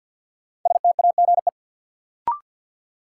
Morse code, heard as a keyed tone, about 52 wpm, 700 Hz. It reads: STAGE
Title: Sentences from Top 1200 Words - New Word Drill Alt - 50wpm